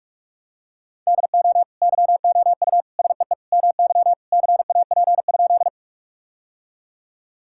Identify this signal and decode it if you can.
Morse code, heard as a keyed tone, about 46 wpm, 700 Hz. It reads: DO YOU SEE MY CAP?
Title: Sentences from Top 1600 Words 45wpm